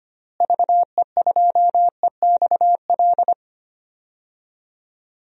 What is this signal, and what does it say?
Morse code, heard as a keyed tone, about 25 wpm, 700 Hz. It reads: VE2EXL
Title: DXCC-100 Call Signs - 6 Characters - Repeated 3x - 25wpm